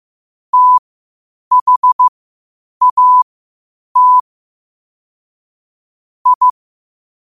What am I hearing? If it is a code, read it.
Morse code, heard as a keyed tone, about 15 wpm, spaced as for 8 wpm, 1000 Hz. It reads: THAT I
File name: qotdfiles / 8wpm